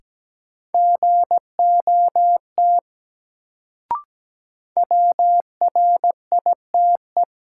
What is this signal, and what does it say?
Morse code, heard as a keyed tone, about 17 wpm, 700 Hz. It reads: GOT WRITE
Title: Sentences from Top 300 Words - New Word Drill - 17wpm